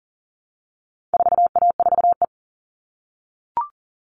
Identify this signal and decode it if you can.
Morse code, heard as a keyed tone, about 40 wpm, 700 Hz. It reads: 4A4E
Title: All Cut Numbers - 4-Digit 40wpm